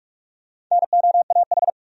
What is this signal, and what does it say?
Morse code, heard as a keyed tone, about 45 wpm, 700 Hz. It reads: NOAH
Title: USA Common Names - Max 4 Letters - 45wpm